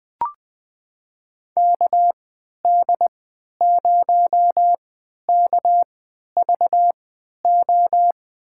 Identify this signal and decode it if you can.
Morse code, heard as a keyed tone, about 20 wpm, spaced as for 11 wpm, 700 Hz. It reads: KD0KVO
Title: USA Call Signs - 6 Characters - 3x Character Spacing - 20wpm